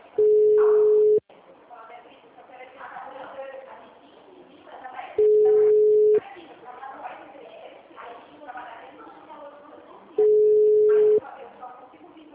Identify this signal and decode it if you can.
Ringback tone (Europe, 425 Hz), with speech.
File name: out- -2002- - - .51668